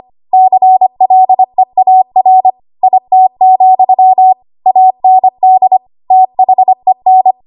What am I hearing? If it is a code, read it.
Morse code, heard as a keyed tone, about 25 wpm, 750 Hz. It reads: CLEAR IT, AND THEN
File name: Archive / 25 WPM / 25WPM